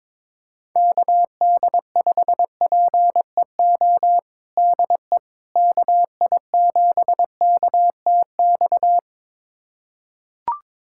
Morse code, heard as a keyed tone, about 22 wpm, 700 Hz. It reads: KD5PEO DE KI7KTX
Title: USA Callsign DE Callsign - 6 Characters - 22wpm